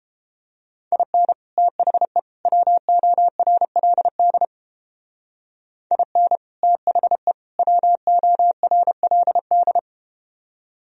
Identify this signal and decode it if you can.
Morse code, heard as a keyed tone, about 30 wpm, 700 Hz. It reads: IN THE WORLD IN THE WORLD